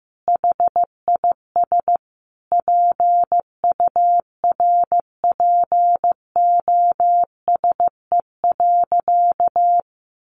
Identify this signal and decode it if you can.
Morse code, heard as a keyed tone, about 15 wpm, 700 Hz. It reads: HIS PURPOSE.
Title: Sentences from Top 1400 Words 15wpm